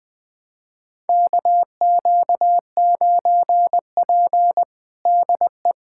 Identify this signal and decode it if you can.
Morse code, heard as a keyed tone, about 20 wpm, 700 Hz. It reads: KQ9P DE
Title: USA Callsign DE Callsign - 4 Characters - 20wpm